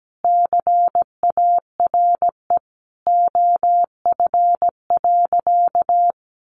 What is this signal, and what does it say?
Morse code, heard as a keyed tone, about 17 wpm, 700 Hz. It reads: CARE OF.
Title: Easy Phrases 17wpm